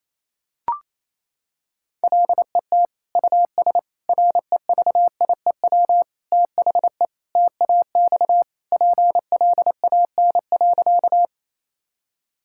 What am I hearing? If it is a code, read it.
Morse code, heard as a keyed tone, about 27 wpm, 700 Hz. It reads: LET US REVIEW THE TAX PLAN.
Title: Sentences from Top 1000 Words - Review Part 6 - 28wpm